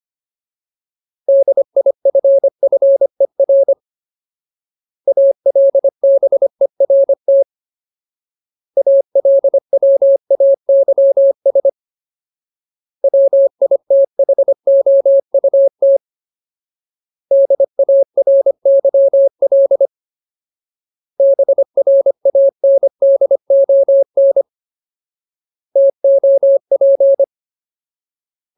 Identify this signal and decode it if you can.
Morse code, heard as a keyed tone, about 25 wpm, 550 Hz. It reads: DIFFER ALBERT ALWAYS WITHOUT DARYL BRANDON TOP
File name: Ultimate Word List / 25WPM Ultimate